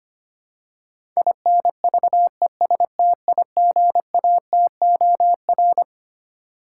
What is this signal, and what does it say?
Morse code, heard as a keyed tone, about 25 wpm, 700 Hz. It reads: INVESTIGATOR
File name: Twelve Letter Words - 25wpm